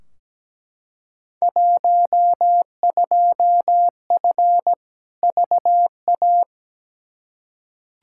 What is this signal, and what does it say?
Morse code, heard as a keyed tone, about 17 wpm, 700 Hz. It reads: 12F VA
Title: Field Day Exchanges Encore - 17wpm